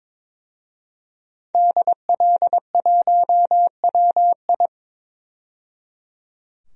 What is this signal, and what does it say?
Morse code, heard as a keyed tone, about 22 wpm, 700 Hz. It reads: DL1WI
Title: DXCC-100 Call Signs - 5 Characters - 22wpm